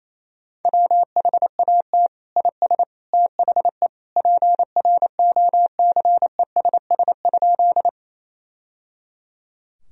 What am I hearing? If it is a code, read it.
Morse code, heard as a keyed tone, about 28 wpm, 700 Hz. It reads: WHAT IS THE PROCESS?
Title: Sentences from Top 1000 Words - Review Part 3 - 28wpm